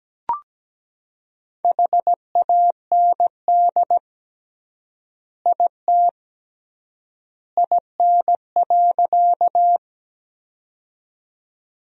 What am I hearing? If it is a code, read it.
Morse code, heard as a keyed tone, about 17 wpm, 700 Hz. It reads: HAND IT IN.
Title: 3x Word-Spacing - Sentences from Top 200 Words - 17wpm